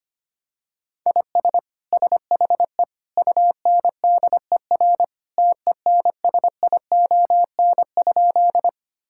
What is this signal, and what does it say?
Morse code, heard as a keyed tone, about 25 wpm, 700 Hz. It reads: IS SHE UNDER TENSION?
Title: Sentences from Top 1600 Words 25wpm